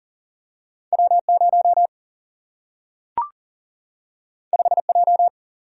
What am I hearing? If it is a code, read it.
Morse code, heard as a keyed tone, about 40 wpm, 700 Hz. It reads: W 0 5 J
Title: Single-Letter Single-Number 40wpm